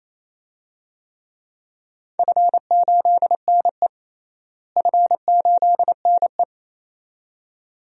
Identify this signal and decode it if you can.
Morse code, heard as a keyed tone, about 28 wpm, 700 Hz. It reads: F8NE F8NE
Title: DXCC-100 Call Signs - 4 Characters - Repeated 2x - 28wpm